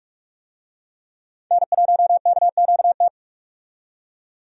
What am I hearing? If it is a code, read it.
Morse code, heard as a keyed tone, about 45 wpm, 700 Hz. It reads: N1KQT
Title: USA Call Signs - 5 Characters - 45wpm